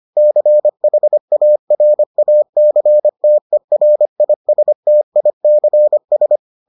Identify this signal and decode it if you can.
Morse code, heard as a keyed tone, about 25 wpm, 600 Hz. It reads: CHARACTERISTICS